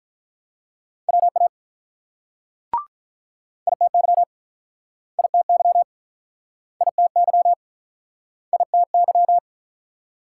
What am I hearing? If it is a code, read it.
Morse code, heard as a keyed tone, about 48 wpm, 700 Hz. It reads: WA ITY ITY ITY ITY